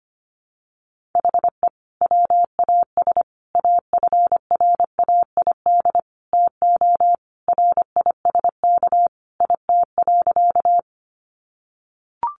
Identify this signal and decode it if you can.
Morse code, heard as a keyed tone, about 25 wpm, 700 Hz. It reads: HE WAS AFRAID TO RISK IT.
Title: Sentences from Top 1500 Words 25wpm